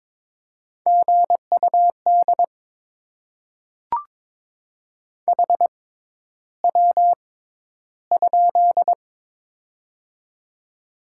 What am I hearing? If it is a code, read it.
Morse code, heard as a keyed tone, about 22 wpm, spaced as for 11 wpm, 700 Hz. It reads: GUD HW?
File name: Top 20 QSO Elements - 6x Character Spacing - 22wpm